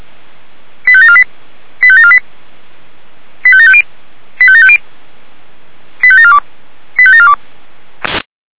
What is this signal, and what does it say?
Five-tone selective call (ZVEI), heard as quick runs of five tones, tones of about 70 ms. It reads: 85748 85748 85750 85750 85742 85742
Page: /464/SDR/ZVEI/85751/2021.07.09/